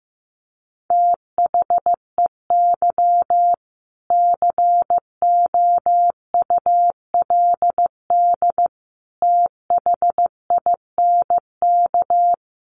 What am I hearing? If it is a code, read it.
Morse code, heard as a keyed tone, about 14 wpm, 700 Hz. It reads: THEY COULD THINK
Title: Sentences from Top 800 Words 15wpm